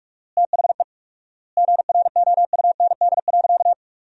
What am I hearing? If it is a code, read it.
Morse code, heard as a keyed tone, about 47 wpm, 690 Hz. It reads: THE GROUND.
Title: 4x Word-Spacing - Sentences from Top 500 Words - 45wpm